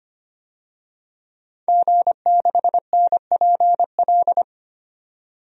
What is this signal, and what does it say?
Morse code, heard as a keyed tone, about 25 wpm, 700 Hz. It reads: G6NPL
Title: DXCC-100 Call Signs - 5 Characters - 25wpm